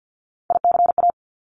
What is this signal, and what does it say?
Morse code, heard as a keyed tone, about 50 wpm, 700 Hz. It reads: ICA
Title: Top Three Letter Combinations in Words 50wpm